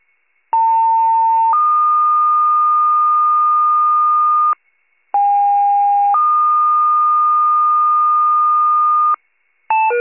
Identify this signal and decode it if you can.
Two-tone page, heard as a one-second tone, then a three-second tone, about 900 then 1220 Hz, about 810 then 1220 Hz. It is heard from 0.5 to 9.2 s.
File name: pager / 73 RESCUE / wcalert